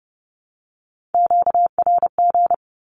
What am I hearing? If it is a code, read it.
Morse code, heard as a keyed tone, about 30 wpm, 700 Hz. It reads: QRG
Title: Q-Codes 30wpm